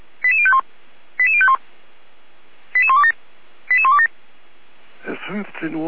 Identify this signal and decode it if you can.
Five-tone selective call (ZVEI), heard as quick runs of five tones, tones of about 70 ms, with speech. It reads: 88051 88051 88127 88127